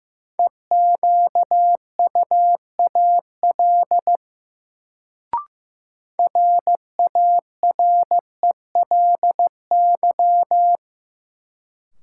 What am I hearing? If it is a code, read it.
Morse code, heard as a keyed tone, about 15 wpm, 700 Hz. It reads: EQUAL RARELY
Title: Sentences from Top 1400 Words - New Word Drill - 15wpm